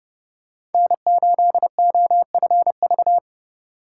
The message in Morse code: N8OFV

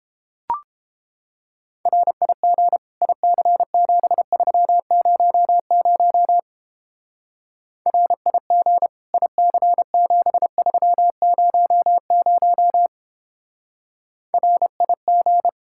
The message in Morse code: RIG IC7300 RIG IC7300 RIG

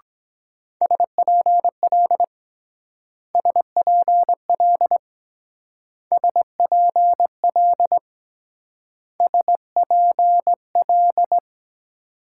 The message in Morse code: SPL SPL SPL SPL

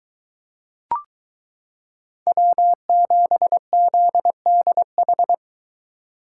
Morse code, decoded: W7ZDH

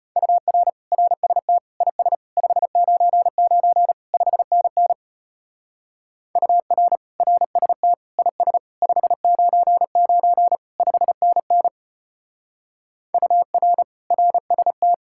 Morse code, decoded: UR RST IS 599 5NN UR RST IS 599 5NN UR RST

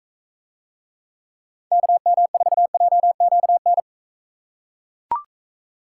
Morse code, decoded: KM4JQN

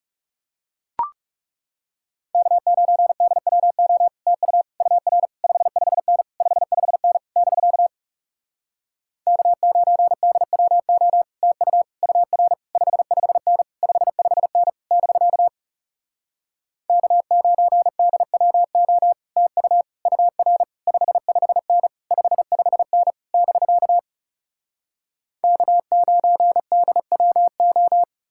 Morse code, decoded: K9DWO TU UR 55N 55N <BK> K9DWO TU UR 55N 55N <BK> K9DWO TU UR 55N 55N <BK> K9DWO

